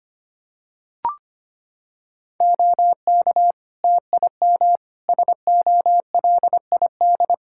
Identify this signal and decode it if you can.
Morse code, heard as a keyed tone, about 25 wpm, 700 Hz. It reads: OK TIM SOLID
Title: Practice QSO - Exchange 3 - More Info - 25wpm